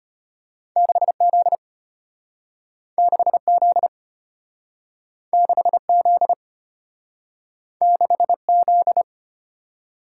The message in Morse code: 6Z 6Z 6Z 6Z

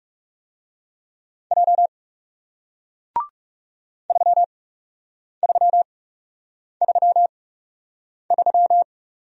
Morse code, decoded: J 3333

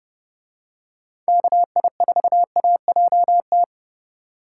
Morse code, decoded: KI4AJT

